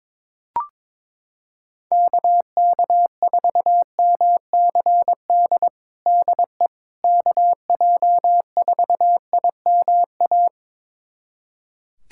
Morse code, heard as a keyed tone, about 22 wpm, 700 Hz. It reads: KK4MCD DE KJ4IMA
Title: USA Callsign DE Callsign - 6 Characters - 22wpm